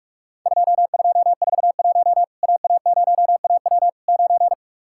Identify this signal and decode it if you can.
Morse code, heard as a keyed tone, about 44 wpm, 700 Hz. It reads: 2241 AA0AW 9
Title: Worked All Europe DX Contest - QTC Exchange - 45wpm